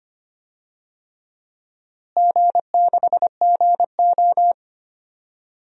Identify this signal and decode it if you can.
Morse code, heard as a keyed tone, about 25 wpm, 700 Hz. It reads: G6GO
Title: DXCC-100 Call Signs - 4 Characters - Repeated 3x - 25wpm